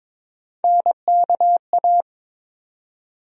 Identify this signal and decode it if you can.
Morse code, heard as a keyed tone, about 21 wpm, 700 Hz. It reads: NKA
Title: USA Call Sign Postfixes - 3 Letters 22wpm